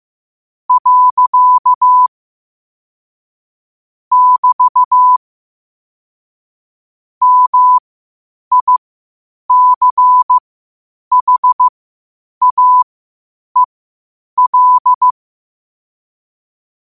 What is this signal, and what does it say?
Morse code, heard as a keyed tone, about 15 wpm, spaced as for 8 wpm, 1000 Hz. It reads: . <BT> MICHAEL
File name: qotdfiles / 8wpm